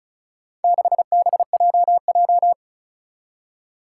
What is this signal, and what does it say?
Morse code, heard as a keyed tone, about 34 wpm, 700 Hz. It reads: 6BJJ
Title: Four Challenging Letter-Number Combinations 35wpm